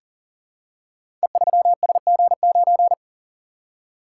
Morse code, heard as a keyed tone, about 39 wpm, 700 Hz. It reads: E3SG9
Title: Code Groups - 5 Letter:Numbers - 40wpm